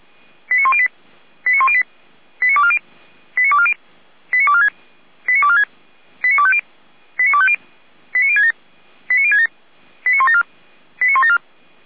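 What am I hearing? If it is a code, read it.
Five-tone selective call (ZVEI), heard as quick runs of five tones, tones of about 70 ms. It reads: 89108 89108 89240 89240 89246 89246 89250 89250 89067 89067 89174 89174
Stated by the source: SDR receiver recording of ZVEI five-tone selective call, code 89337, for Feuerwehr Orlamünde fire brigade.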